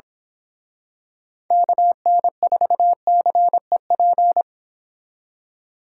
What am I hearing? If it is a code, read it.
Morse code, heard as a keyed tone, about 27 wpm, 700 Hz. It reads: KN4CEP
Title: Speed Racing - USA Call Signs 22wpm